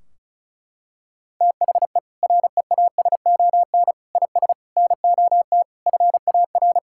Morse code, heard as a keyed tone, about 36 wpm, 700 Hz. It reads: THE REASON IS NOT FAR
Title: Sentences from Top 1600 Words 35wpm